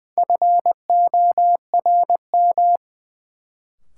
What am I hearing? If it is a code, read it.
Morse code, heard as a keyed tone, about 21 wpm, 700 Hz. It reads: FORM
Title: Sentences from Top 1000 Words - Review Part 1 - New Word Drill - 20wpm